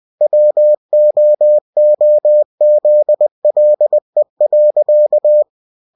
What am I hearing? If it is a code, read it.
Morse code, heard as a keyed tone, about 20 wpm, 600 Hz. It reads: WOOZLE.
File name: cw-books / winnie-the-pooh / winnie-20wpm-page-0022